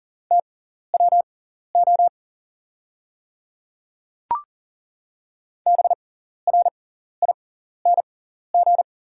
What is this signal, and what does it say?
Morse code, heard as a keyed tone, about 40 wpm, spaced as for 14 wpm, 690 Hz. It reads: TWO BRING
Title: 6x Character Spacing - Top 500 Words - 40wpm